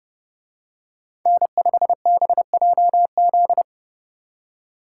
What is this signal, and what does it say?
Morse code, heard as a keyed tone, about 29 wpm, 700 Hz. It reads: N5BJZ
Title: Speed Racing - USA 1x3 Call Signs 30wpm